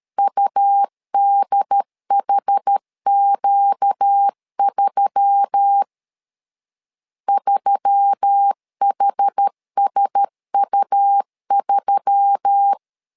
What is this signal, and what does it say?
Morse code, heard as a keyed tone, about 13 wpm, 800 Hz. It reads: UDHQ3 3HSU3
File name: LEZIONI / 4 Q1S3 12wpm 2023